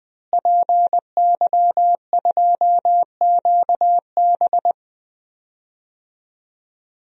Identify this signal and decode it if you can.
Morse code, heard as a keyed tone, about 20 wpm, 700 Hz. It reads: PY2QB
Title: DXCC-100 Call Signs - 5 Characters - Repeated 2x - 20wpm